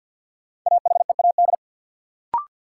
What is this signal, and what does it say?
Morse code, heard as a keyed tone, about 50 wpm, 700 Hz. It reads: AHEAD
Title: Sentences from Top 1000 Words - Review Part 6 - New Word Drill - 50wpm